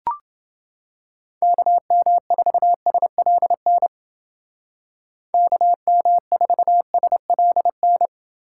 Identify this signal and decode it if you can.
Morse code, heard as a keyed tone, about 28 wpm, 700 Hz. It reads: KM4SLN KM4SLN